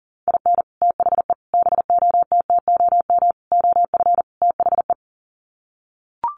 Morse code, heard as a keyed tone, about 40 wpm, 700 Hz. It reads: IN THE BOTTOM OF THE
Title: Sets of 5 Words Encore 40wpm